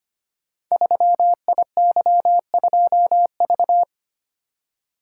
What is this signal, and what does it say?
Morse code, heard as a keyed tone, about 25 wpm, 700 Hz. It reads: 3IY2V
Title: Code Groups - 5 Letter:Numbers - 25wpm